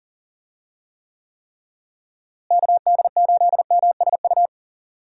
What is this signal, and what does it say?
Morse code, heard as a keyed tone, about 40 wpm, 700 Hz. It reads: KD8MSU
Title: USA Call Signs - 6 Characters - Repeated 3x - 40wpm